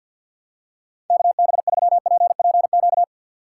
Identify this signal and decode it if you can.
Morse code, heard as a keyed tone, about 51 wpm, 690 Hz. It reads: KB3PPQ
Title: USA Call Signs - 3x Character Spacing - 50wpm